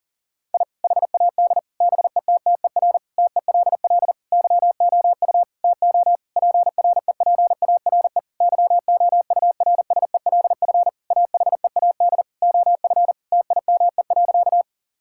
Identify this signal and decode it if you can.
Morse code, heard as a keyed tone, about 40 wpm, 700 Hz. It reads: I HAD BETTER TELL YOU TO PREPARE YOURSELF AHEAD OF TIME.
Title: Sentences from Top 1000 Words - Review Part 6 - 40wpm